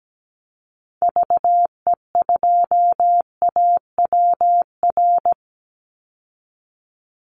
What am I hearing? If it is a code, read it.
Morse code, heard as a keyed tone, about 17 wpm, 700 Hz. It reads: VE2AWR
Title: Call Signs Repeated 3x - 17wpm